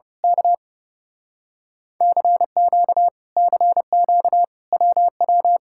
Morse code, heard as a keyed tone, about 28 wpm, 700 Hz. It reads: K CQ CQ WW
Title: Speed-Racing Common QSO Phrases - 30wpm